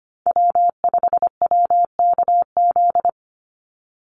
Morse code, heard as a keyed tone, about 25 wpm, 700 Hz. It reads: W5WKZ
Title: USA Call Signs 25wpm